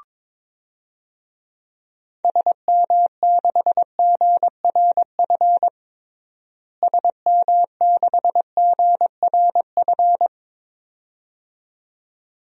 Morse code, heard as a keyed tone, about 22 wpm, 700 Hz. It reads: SM6GRF SM6GRF
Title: DXCC-100 Call Signs - 6 Characters - Repeated 2x - 22wpm